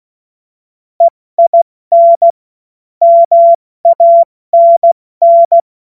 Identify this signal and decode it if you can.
Morse code, heard as a keyed tone, about 16 wpm, 680 Hz. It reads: EIN MANN
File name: Morsekurs / Witze / Witz 19. April 2022 70 F80